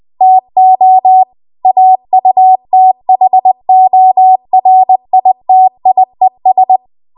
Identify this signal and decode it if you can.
Morse code, heard as a keyed tone, about 20 wpm, 750 Hz. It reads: TO AUTHORITIES
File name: Archive / 20 WPM / 20WPM